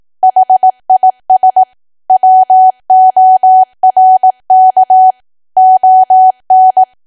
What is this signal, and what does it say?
Morse code, heard as a keyed tone, about 18 wpm, 750 Hz. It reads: HIS WORK ON
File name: Archive / 18 WPM / 18WPM